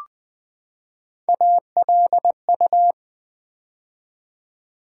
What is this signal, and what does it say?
Morse code, heard as a keyed tone, about 20 wpm, 700 Hz. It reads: ALU
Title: Build A Word - 3 Letters At A Time - 8-Letter Words - 20wpm